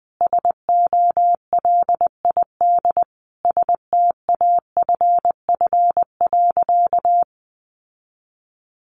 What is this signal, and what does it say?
Morse code, heard as a keyed tone, about 20 wpm, 700 Hz. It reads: SOLID STAFF.
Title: Sentences from Top 1500 Words 20wpm